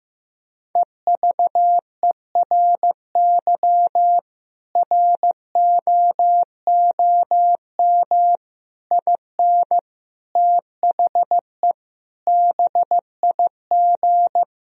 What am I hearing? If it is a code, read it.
Morse code, heard as a keyed tone, about 15 wpm, 700 Hz. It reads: EVERY ROOM IN THE BIG
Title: Sentences from Top 1000 Words - Review Part 3 - 15wpm